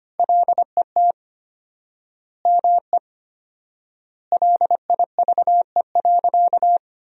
Morse code, heard as a keyed tone, about 25 wpm, 700 Hz. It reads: LET ME LIVE.
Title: 4x Word-Spacing - Sentences from Top 300 Words - 25wpm